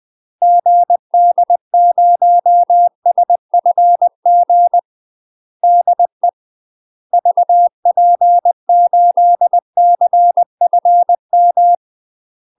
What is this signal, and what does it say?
Morse code, heard as a keyed tone, about 20 wpm, 700 Hz. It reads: GD0SFG DE VP8CFM